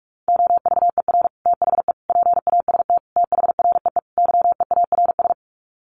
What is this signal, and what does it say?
Morse code, heard as a keyed tone, about 44 wpm, 700 Hz. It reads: OVER THE PAST THREE YEARS
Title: Sets of 5 Words Encore2 45wpm